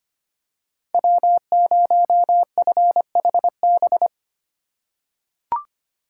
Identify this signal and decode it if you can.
Morse code, heard as a keyed tone, about 25 wpm, 700 Hz. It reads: W0FHB
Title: USA Call Signs - 5 Characters - 4x Character Spacing - 25wpm